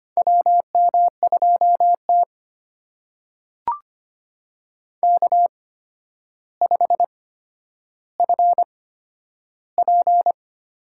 Morse code, heard as a keyed tone, about 25 wpm, spaced as for 11 wpm, 700 Hz. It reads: WM2T K5FP
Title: USA Call Signs - 4 Characters - 8x Character Spacing - 25wpm